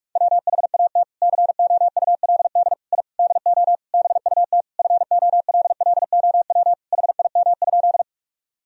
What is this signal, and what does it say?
Morse code, heard as a keyed tone, about 41 wpm, 700 Hz. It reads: WHAT COULD I DO BUT FOLLOW HIM?
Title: Sentences from Top 1000 Words - Review Part 1 - 45wpm